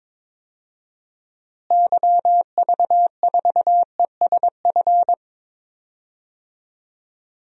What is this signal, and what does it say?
Morse code, heard as a keyed tone, about 22 wpm, 700 Hz. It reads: YV4ESF
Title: DXCC-100 Call Signs - 6 Characters - Repeated 3x - 22wpm